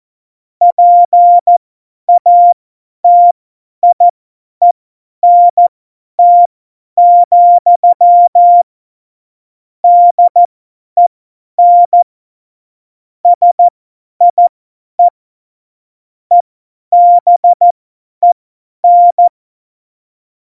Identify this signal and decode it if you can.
Morse code, heard as a keyed tone, about 14 wpm, spaced as for 10 wpm, 700 Hz. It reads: PATIENT, DEN SIE EBEN
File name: Morsekurs / Witze / Witz 01. September 2020, 50 F70